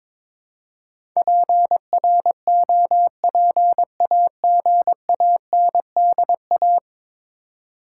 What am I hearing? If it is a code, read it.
Morse code, heard as a keyed tone, about 21 wpm, 700 Hz. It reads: PROPAGANDA
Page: morse-code-ninja-practice-22wpm